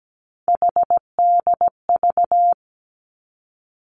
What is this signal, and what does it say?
Morse code, heard as a keyed tone, about 17 wpm, 700 Hz. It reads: HDV
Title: Three Challenging Letter-Number Combinations 17wpm